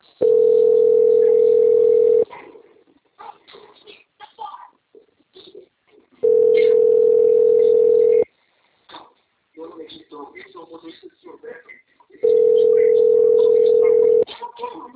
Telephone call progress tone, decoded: ringback tone

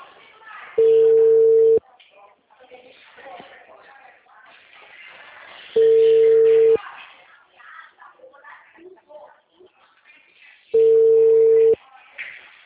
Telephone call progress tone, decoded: ringback tone

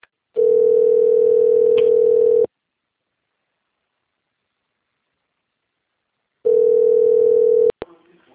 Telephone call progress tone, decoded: ringback tone

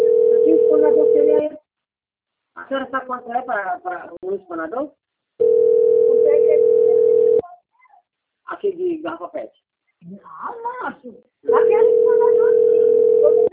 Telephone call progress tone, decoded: ringback tone

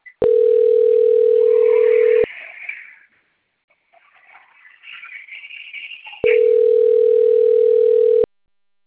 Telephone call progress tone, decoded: ringback tone

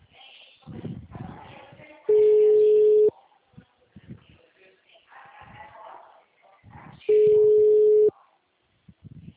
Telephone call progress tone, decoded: ringback tone

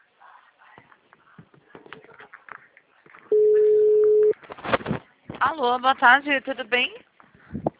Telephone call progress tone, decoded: ringback tone

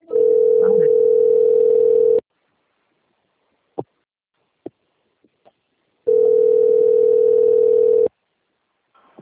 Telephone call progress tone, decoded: ringback tone

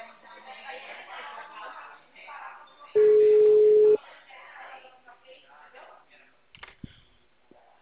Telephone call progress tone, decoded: ringback tone

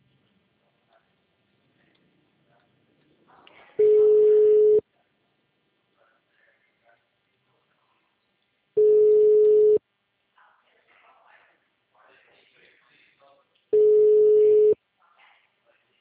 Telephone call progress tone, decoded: ringback tone

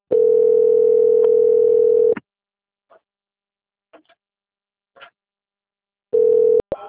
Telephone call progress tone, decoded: ringback tone

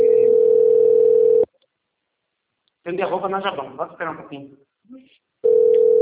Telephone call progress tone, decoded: ringback tone

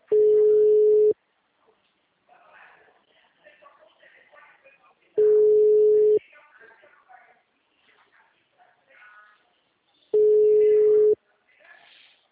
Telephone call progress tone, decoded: ringback tone

